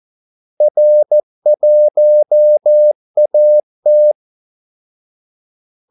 Morse code, heard as a keyed tone, about 14 wpm, 600 Hz. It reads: R1AT
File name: Callsigns-14wpm